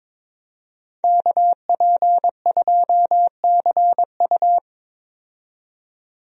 Morse code, heard as a keyed tone, about 22 wpm, 700 Hz. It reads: KP2CU
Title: DXCC-100 Call Signs - 5 Characters - 22wpm